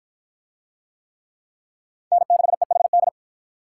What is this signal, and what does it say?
Morse code, heard as a keyed tone, about 53 wpm, 700 Hz. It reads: N6EHD